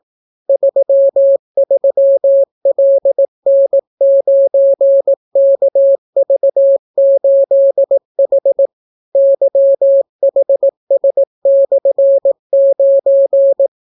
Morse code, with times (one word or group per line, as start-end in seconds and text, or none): 0.5-8.7 33LN9KV8H
9.2-13.7 YHS/9